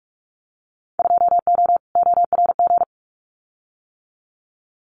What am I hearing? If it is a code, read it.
Morse code, heard as a keyed tone, about 46 wpm, 700 Hz. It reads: 2O ORG
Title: Winter Field Day Exchanges 45wpm